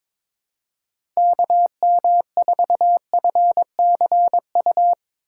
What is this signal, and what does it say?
Morse code, heard as a keyed tone, about 22 wpm, 700 Hz. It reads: KM4FCU
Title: USA Call Signs - 6 Characters - 4x Character Spacing - 22wpm